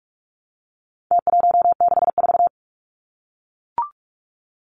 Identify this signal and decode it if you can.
Morse code, heard as a keyed tone, about 44 wpm, 700 Hz. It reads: T164
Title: Numbers - Contest Style - 4-Digit 45wpm